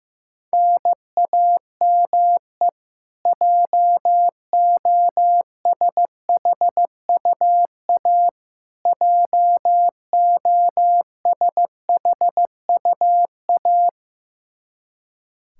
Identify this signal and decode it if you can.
Morse code, heard as a keyed tone, about 15 wpm, 700 Hz. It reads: NAME JOSHUA JOSHUA
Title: Practice QSO - Exchange 1 - Beginning - 15wpm